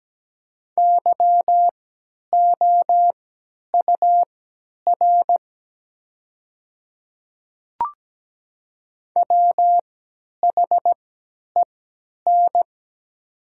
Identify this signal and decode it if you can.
Morse code, heard as a keyed tone, about 17 wpm, spaced as for 10 wpm, 700 Hz. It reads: YOUR WHEN
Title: 3x Character Spacing - Top 100 Words - 17wpm